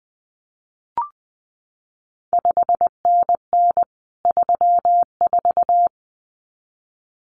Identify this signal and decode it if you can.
Morse code, heard as a keyed tone, about 19 wpm, 700 Hz. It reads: 5NN 34
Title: CQ WW DX Contest Exchange Encore - 20wpm